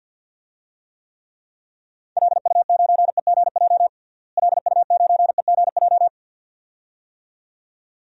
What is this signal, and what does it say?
Morse code, heard as a keyed tone, about 52 wpm, 700 Hz. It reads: LU9EGJ LU9EGJ
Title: DXCC-100 Call Signs - 6 Characters - Repeated 2x - 50wpm